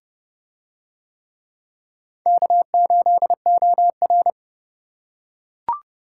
Morse code, heard as a keyed tone, about 30 wpm, 700 Hz. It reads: K8OR